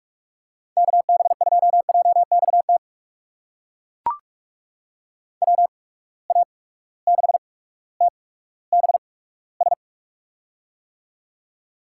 Morse code, heard as a keyed tone, about 42 wpm, spaced as for 19 wpm, 700 Hz. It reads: KB2JXT WA6TBS
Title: USA Call Signs - 6 Characters - 8x Character Spacing - 45wpm